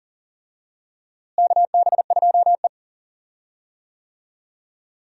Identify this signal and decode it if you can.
Morse code, heard as a keyed tone, about 40 wpm, 700 Hz. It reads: KB2E
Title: USA Call Signs - 4 Characters - Repeated 3x - 40wpm